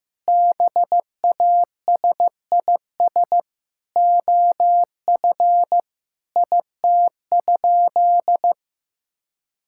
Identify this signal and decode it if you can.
Morse code, heard as a keyed tone, about 16 wpm, 700 Hz. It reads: BASIS OF IT?
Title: Sentences from Top 1000 Words - Review Part 5 - 15wpm